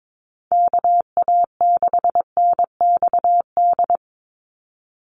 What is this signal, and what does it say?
Morse code, heard as a keyed tone, about 22 wpm, 700 Hz. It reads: KA6NXD
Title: USA Call Signs 22wpm